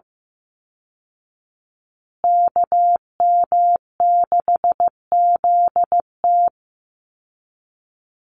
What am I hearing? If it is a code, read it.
Morse code, heard as a keyed tone, about 15 wpm, 700 Hz. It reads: KM6ZT
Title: Call Signs Repeated 2x - 15wpm